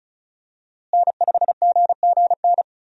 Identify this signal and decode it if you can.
Morse code, heard as a keyed tone, about 36 wpm, 700 Hz. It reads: N5GGN
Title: USA Call Signs - 5 Characters - Repeated 3x - 35wpm